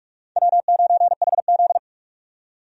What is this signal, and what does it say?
Morse code, heard as a keyed tone, about 45 wpm, 700 Hz. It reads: W9HZ
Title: USA Call Signs - 4 Characters - 45wpm